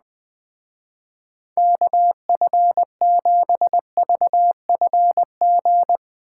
Morse code, heard as a keyed tone, about 20 wpm, 700 Hz. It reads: KF7VFG
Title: Speed Racing - USA 2x3 Call Signs 17wpm